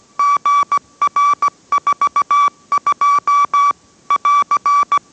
Morse code, heard as a keyed tone, about 17 wpm, 1150 Hz. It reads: GR42 <AR>